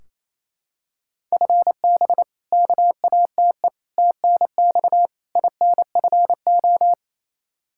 Morse code, heard as a keyed tone, about 28 wpm, 700 Hz. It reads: FB KATE TNX INFO
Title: Practice QSO - Exchange 3 - More Info - 28wpm